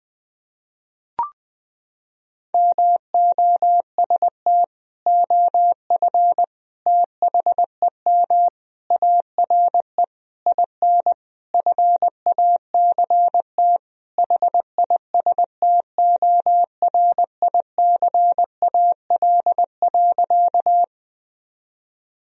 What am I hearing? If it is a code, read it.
Morse code, heard as a keyed tone, about 20 wpm, 700 Hz. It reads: MOST OF THEM ARE IN FACT HISTORICAL.
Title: Sentences from Top 1000 Words - Review Part 4 - 20wpm